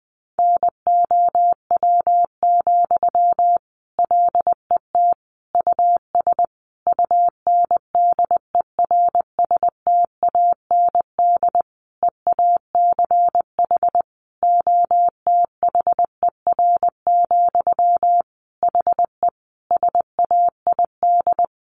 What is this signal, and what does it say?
Morse code, heard as a keyed tone, about 20 wpm, 700 Hz. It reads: NOW, LET US UNDERSTAND EACH OTHER, HE SAID
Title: Sentences from Top 1500 Words 20wpm